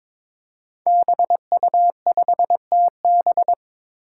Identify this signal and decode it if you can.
Morse code, heard as a keyed tone, about 22 wpm, 700 Hz. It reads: BU5TB
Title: Five Challenging Letter-Number Combinations 22wpm